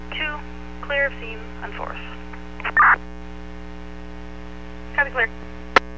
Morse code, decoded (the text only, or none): none